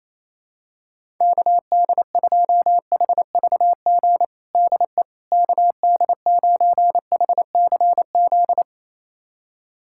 Morse code, heard as a keyed tone, about 28 wpm, 700 Hz. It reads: KD2HVG DE KD9HCZ
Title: USA Callsign DE Callsign - 6 Characters - 28wpm